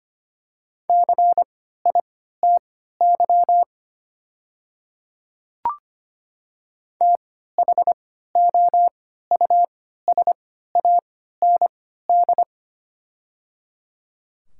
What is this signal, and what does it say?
Morse code, heard as a keyed tone, about 25 wpm, spaced as for 14 wpm, 700 Hz. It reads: CITY THOUSAND
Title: 3x Character Spacing - Top 500 Words - 25wpm